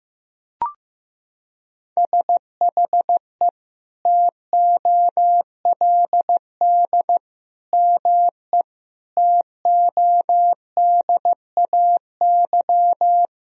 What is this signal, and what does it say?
Morse code, heard as a keyed tone, about 15 wpm, 700 Hz. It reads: SHE TOLD ME TODAY
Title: Sentences from Top 1000 Words - Review Part 4 - 15wpm